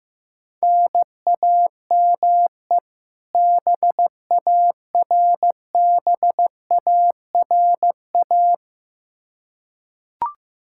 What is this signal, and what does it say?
Morse code, heard as a keyed tone, about 15 wpm, 700 Hz. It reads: NAME BARBARA
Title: Practice QSO - Exchange 1 - Beginning - 15wpm